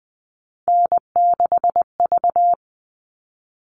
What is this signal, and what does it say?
Morse code, heard as a keyed tone, about 20 wpm, 700 Hz. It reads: N6V
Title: All Cut Numbers - 3-Digit 20wpm